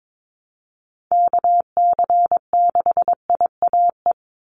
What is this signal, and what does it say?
Morse code, heard as a keyed tone, about 22 wpm, 700 Hz. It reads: KC6IAE
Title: USA Call Signs 22wpm